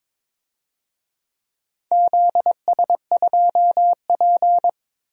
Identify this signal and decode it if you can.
Morse code, heard as a keyed tone, about 21 wpm, 700 Hz. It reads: ZS2P
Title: DXCC-100 Call Signs - 4 Characters - Repeated 2x - 22wpm